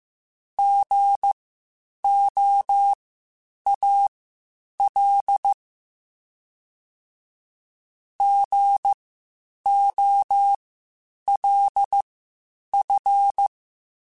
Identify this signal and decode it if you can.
Morse code, heard as a keyed tone, about 15 wpm, spaced as for 8 wpm, 800 Hz. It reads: GOAL GOLF
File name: Spaced Alphas / Spaced Alphas mp3s / Gs sp @13